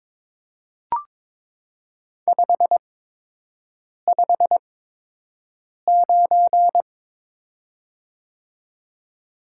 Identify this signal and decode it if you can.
Morse code, heard as a keyed tone, about 22 wpm, spaced as for 6 wpm, 700 Hz. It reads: 559
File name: Top 90 QSO Elements - 8x Character Spacing - 22wpm